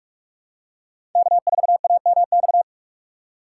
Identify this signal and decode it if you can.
Morse code, heard as a keyed tone, about 43 wpm, 700 Hz. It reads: K4AMX
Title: USA Call Signs - 6x Character Spacing - 45wpm